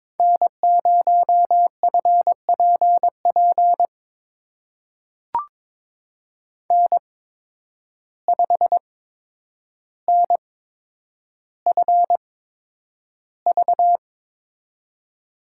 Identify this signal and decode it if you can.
Morse code, heard as a keyed tone, about 22 wpm, spaced as for 9 wpm, 700 Hz. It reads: N0FPP N5NFV
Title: USA Call Signs - 5 Characters - 8x Character Spacing - 22wpm